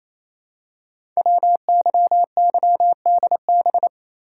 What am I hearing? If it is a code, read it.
Morse code, heard as a keyed tone, about 29 wpm, 700 Hz. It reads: WYYDB
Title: Code Groups - 5 Letters - 28wpm